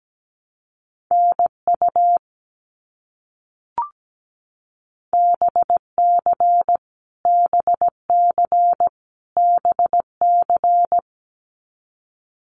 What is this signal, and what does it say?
Morse code, heard as a keyed tone, about 17 wpm, 700 Hz. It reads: NU BC BC BC
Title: Canadian Provinces and Territories - Repeated 3x - 17wpm